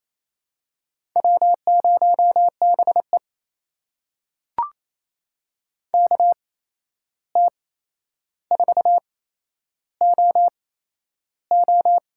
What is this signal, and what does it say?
Morse code, heard as a keyed tone, about 28 wpm, spaced as for 11 wpm, 700 Hz. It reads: W0BE KT4OO
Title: USA Call Signs - 8x Character Spacing - 28wpm